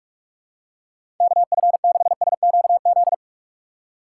Morse code, heard as a keyed tone, about 46 wpm, 700 Hz. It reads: KF6SQZ
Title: USA Call Signs - 3x Character Spacing - 45wpm